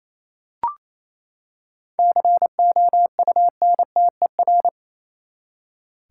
Morse code, heard as a keyed tone, about 28 wpm, 700 Hz. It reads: COUNTER
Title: Seven Letter Words - 28wpm